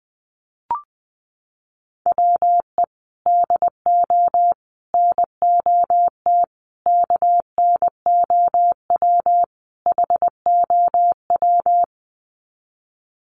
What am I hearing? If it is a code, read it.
Morse code, heard as a keyed tone, about 20 wpm, 700 Hz. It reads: WE DO NOT KNOW HOW
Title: Sets of 5 Words Encore 20wpm